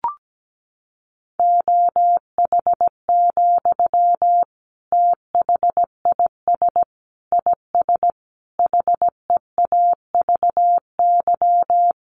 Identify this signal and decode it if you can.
Morse code, heard as a keyed tone, about 17 wpm, 700 Hz. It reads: OH, THIS IS HEAVY
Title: Sentences from Top 1100 Words 17wpm